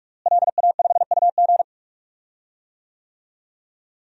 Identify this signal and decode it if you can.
Morse code, heard as a keyed tone, about 46 wpm, 700 Hz. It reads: LA5UG
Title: DXCC-100 Call Signs - 5 Characters - Repeated 2x - 45wpm